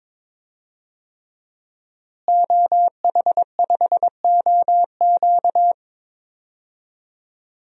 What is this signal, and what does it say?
Morse code, heard as a keyed tone, about 22 wpm, 700 Hz. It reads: OH5OQ
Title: DXCC-100 Call Signs - 5 Characters - Repeated 3x - 22wpm